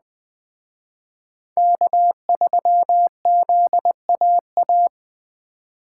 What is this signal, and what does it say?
Morse code, heard as a keyed tone, about 19 wpm, 700 Hz. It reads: K3ZAA